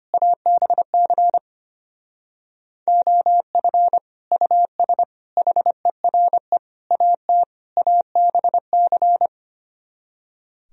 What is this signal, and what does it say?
Morse code, heard as a keyed tone, about 31 wpm, 700 Hz. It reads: ABC OF US HERE AT ABC
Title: Speed-Racing - Sets of 5 Words 25wpm